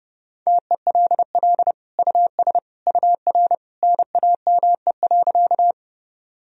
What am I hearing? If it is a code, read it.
Morse code, heard as a keyed tone, about 32 wpm, 700 Hz. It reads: TELL US UR NAME.
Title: Sentences from Top 200 Words and Abbreviations 30wpm